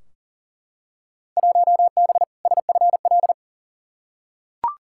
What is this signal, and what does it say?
Morse code, heard as a keyed tone, about 37 wpm, 700 Hz. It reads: 1B SFL
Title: Field Day Exchanges Encore - 40wpm